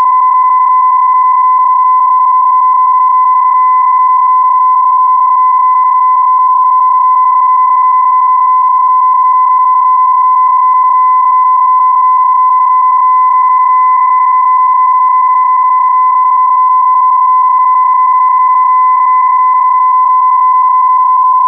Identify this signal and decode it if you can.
PSK31, about 31.25 baud.